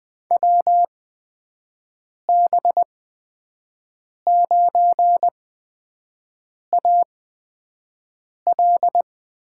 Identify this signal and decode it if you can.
Morse code, heard as a keyed tone, about 20 wpm, spaced as for 5 wpm, 700 Hz. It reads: WB9AL
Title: USA Call Signs - 5 Characters - 8x Character Spacing - 20wpm